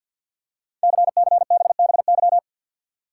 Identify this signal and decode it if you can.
Morse code, heard as a keyed tone, about 52 wpm, 700 Hz. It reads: CCBBY
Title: Code Groups - 5 Letters - 50wpm